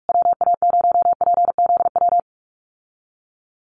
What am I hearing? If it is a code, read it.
Morse code, heard as a keyed tone, about 44 wpm, 700 Hz. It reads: WA0PZW
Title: USA Call Signs Repeated 3x 45wpm